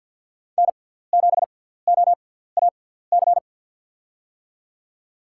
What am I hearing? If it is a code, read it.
Morse code, heard as a keyed tone, about 48 wpm, spaced as for 17 wpm, 700 Hz. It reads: N7OAC
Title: USA Call Signs - 6x Character Spacing - 50wpm